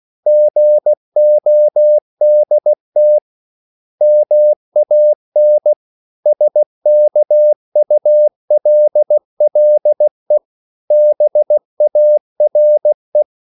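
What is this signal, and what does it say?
Morse code, heard as a keyed tone, about 16 wpm, 600 Hz. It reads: GODT MAN SKULLE BARE
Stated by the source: Slow (QRS) Morse code at 16 wpm, from CW med Gnister.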